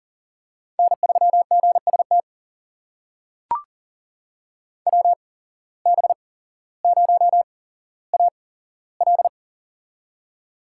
Morse code, heard as a keyed tone, about 40 wpm, spaced as for 17 wpm, 700 Hz. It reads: N3GST WB0AL
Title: USA Call Signs - 5 Characters - 8x Character Spacing - 40wpm